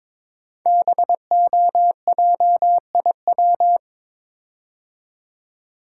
Morse code, heard as a keyed tone, about 22 wpm, 700 Hz. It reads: BOJIW
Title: Code Groups - 5 Letters - 22wpm